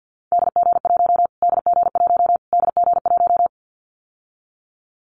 Morse code, heard as a keyed tone, about 49 wpm, 700 Hz. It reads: DG1 DG1 DG1
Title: Callsign Prefixes for Top 25 DXCC - Repeated 3x - 50wpm